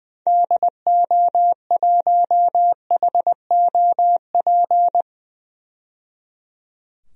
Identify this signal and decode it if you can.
Morse code, heard as a keyed tone, about 20 wpm, 700 Hz. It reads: DO1HOP
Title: DXCC-100 Call Signs - 6 Characters - 20wpm